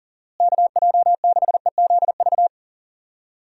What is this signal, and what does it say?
Morse code, heard as a keyed tone, about 40 wpm, 700 Hz. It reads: KJ6EZV
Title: USA Call Signs - 6 Characters - Repeated 2x - 40wpm